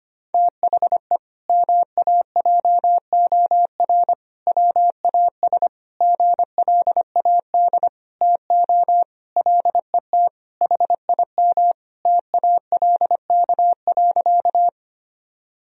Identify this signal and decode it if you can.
Morse code, heard as a keyed tone, about 25 wpm, 700 Hz. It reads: THE MAJOR WAS GLAD TO LET HIM TALK.
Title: Sentences from Top 1000 Words - Review Part 4 - 25wpm